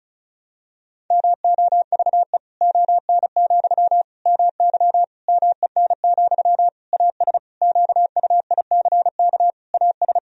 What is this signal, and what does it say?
Morse code, heard as a keyed tone, about 34 wpm, 700 Hz. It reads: MOVE ON, MY MEN, AS QUICK AS
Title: Sentences from Top 500 Words and Abbreviations 35wpm